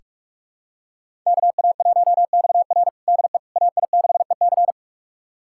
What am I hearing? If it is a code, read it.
Morse code, heard as a keyed tone, about 43 wpm, 700 Hz. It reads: KA1XR DE AI6EC
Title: USA Callsign DE Callsign - 5 Characters - 45wpm